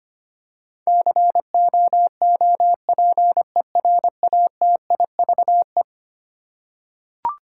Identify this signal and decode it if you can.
Morse code, heard as a keyed tone, about 25 wpm, 700 Hz. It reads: COOPERATIVE
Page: morse-code-ninja-practice-25wpm